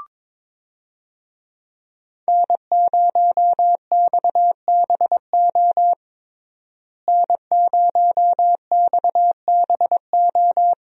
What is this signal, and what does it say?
Morse code, heard as a keyed tone, about 22 wpm, 700 Hz. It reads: N0XBO N0XBO
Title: USA Call Signs - 5 Characters - Repeated 2x - 22wpm